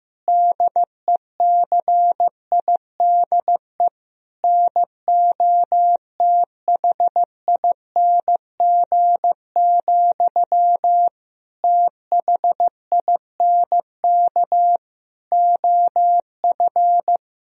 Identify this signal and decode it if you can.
Morse code, heard as a keyed tone, about 15 wpm, 700 Hz. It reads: DECIDE NOTHING, THINK OF
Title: Sentences from Top 1000 Words - Review Part 3 - 15wpm